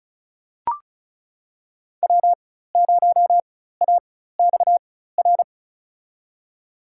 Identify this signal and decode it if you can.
Morse code, heard as a keyed tone, about 36 wpm, spaced as for 16 wpm, 700 Hz. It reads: W0AXR
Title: USA Call Signs - 4x Character Spacing - 35wpm